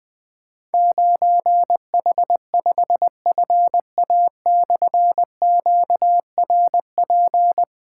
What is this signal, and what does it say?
Morse code, heard as a keyed tone, about 20 wpm, 700 Hz. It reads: 9H5FA/QRP
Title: Difficult Call Signs 20wpm